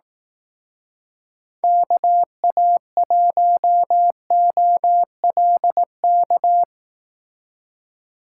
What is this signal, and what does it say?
Morse code, heard as a keyed tone, about 18 wpm, 700 Hz. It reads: KA1OLK
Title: Speed Racing - USA 2x3 Call Signs 15wpm